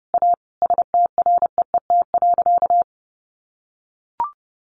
Morse code, heard as a keyed tone, about 30 wpm, 700 Hz. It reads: A STREET.
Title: Sentences from Top 500 Words 30wpm